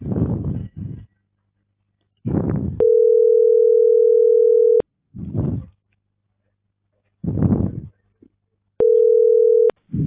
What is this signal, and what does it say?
Ringback tone (North America), 440 + 480 Hz.